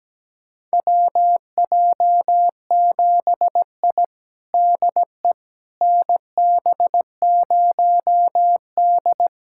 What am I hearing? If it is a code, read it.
Morse code, heard as a keyed tone, about 17 wpm, 700 Hz. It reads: WJ7I DE NB0D
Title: USA Callsign DE Callsign - 4 Characters - 17wpm